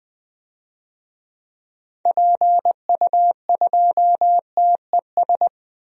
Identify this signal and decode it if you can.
Morse code, heard as a keyed tone, about 20 wpm, 700 Hz. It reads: PU2TES
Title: DXCC-100 Call Signs - 6 Characters - 20wpm